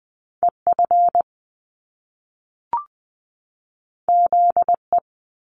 Morse code, heard as a keyed tone, about 20 wpm, 700 Hz. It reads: EF ZE
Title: Top Two Letter Combinations in Words 20wpm